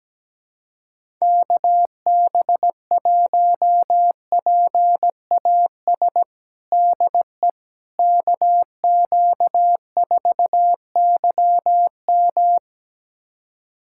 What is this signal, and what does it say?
Morse code, heard as a keyed tone, about 17 wpm, 700 Hz. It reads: KB1PAS DE KQ4YM
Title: USA Callsign DE Callsign 17wpm